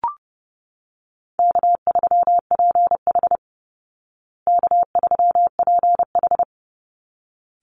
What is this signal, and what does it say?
Morse code, heard as a keyed tone, about 30 wpm, 700 Hz. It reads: K3PH K3PH
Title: Call Signs Repeated 3x - 30wpm